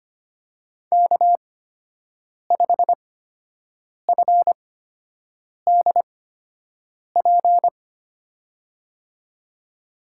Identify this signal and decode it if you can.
Morse code, heard as a keyed tone, about 25 wpm, spaced as for 7 wpm, 700 Hz. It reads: K5FDP
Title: USA Call Signs - 5 Characters - 8x Character Spacing - 25wpm